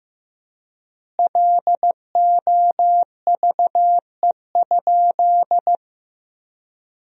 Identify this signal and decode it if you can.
Morse code, heard as a keyed tone, about 15 wpm, 700 Hz. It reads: LOVE?
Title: 4x Word-Spacing - Sentences from Top 500 Words - 15wpm